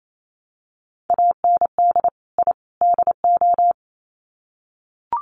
Morse code, heard as a keyed tone, about 28 wpm, 700 Hz. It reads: AND I DO
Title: Sets of 3 Words 28wpm